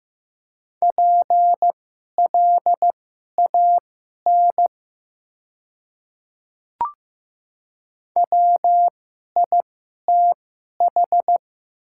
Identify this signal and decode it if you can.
Morse code, heard as a keyed tone, about 15 wpm, spaced as for 11 wpm, 700 Hz. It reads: PLAN WITH
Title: 2x Character Spacing - Top 500 Words - 15wpm